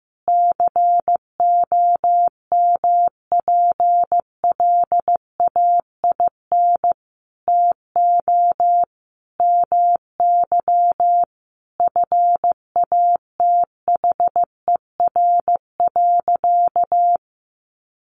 Morse code, heard as a keyed tone, about 14 wpm, 700 Hz. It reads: COMPLAIN TO MY FATHER.
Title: Sentences from Top 1300 Words 15wpm